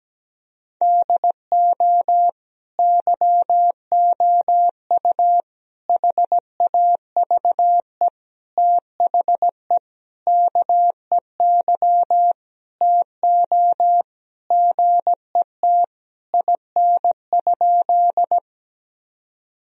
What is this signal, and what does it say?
Morse code, heard as a keyed tone, about 17 wpm, 700 Hz. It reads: DO YOU HAVE THE KEY TO GET IN?
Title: Sentences from Top 1000 Words - Review Part 4 - 17wpm